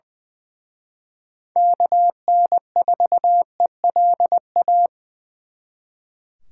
Morse code, heard as a keyed tone, about 24 wpm, 700 Hz. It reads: KN4ELA
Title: Speed Racing - USA 2x3 Call Signs 20wpm